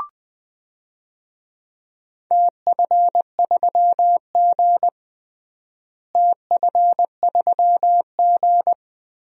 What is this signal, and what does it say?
Morse code, heard as a keyed tone, about 20 wpm, 700 Hz. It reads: TF3G TF3G
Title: DXCC-100 Call Signs - 4 Characters - Repeated 2x - 20wpm